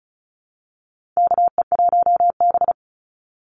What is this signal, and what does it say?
Morse code, heard as a keyed tone, about 34 wpm, 700 Hz. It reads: KE1B
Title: Call Signs Repeated 2x - 35wpm